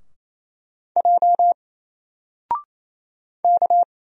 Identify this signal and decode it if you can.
Morse code, heard as a keyed tone, about 28 wpm, 700 Hz. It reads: J K
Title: Single Character (Letter:Number:Punctuation:Prosign) Alt 28wpm